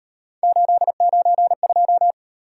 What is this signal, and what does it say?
Morse code, heard as a keyed tone, about 36 wpm, 700 Hz. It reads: 892